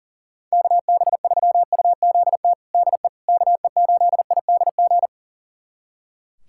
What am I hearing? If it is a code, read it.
Morse code, heard as a keyed tone, about 40 wpm, 700 Hz. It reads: KB3UZT DE KE8IDG
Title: USA Callsign DE Callsign 40wpm